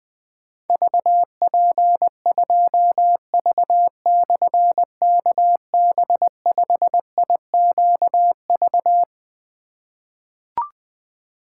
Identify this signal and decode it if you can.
Morse code, heard as a keyed tone, about 20 wpm, 700 Hz. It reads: VP2V/KB5IQV
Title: Difficult Call Signs 20wpm